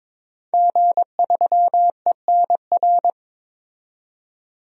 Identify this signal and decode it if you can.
Morse code, heard as a keyed tone, about 22 wpm, 700 Hz. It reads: G3ENR
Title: DXCC-100 Call Signs - 5 Characters - 22wpm